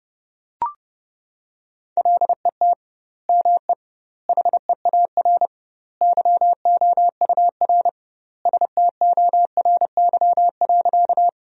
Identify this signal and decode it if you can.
Morse code, heard as a keyed tone, about 30 wpm, 700 Hz. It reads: LET ME HEAR YOUR STORY.
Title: 2x Word-Spacing - Sentences from Top 300 Words - 30wpm